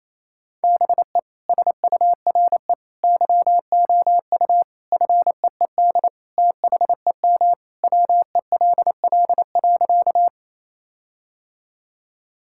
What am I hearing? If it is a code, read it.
Morse code, heard as a keyed tone, about 28 wpm, 700 Hz. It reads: BE SURE YOU FEED THEM WELL.
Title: Sentences from Top 1600 Words 28wpm